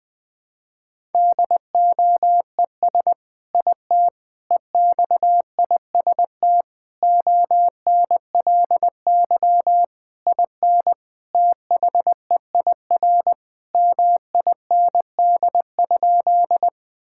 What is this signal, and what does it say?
Morse code, heard as a keyed tone, about 20 wpm, 700 Hz. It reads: DOES IT EXIST ONLY IN THEIR MIND?
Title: Sentences from Top 1000 Words - Review Part 5 - 20wpm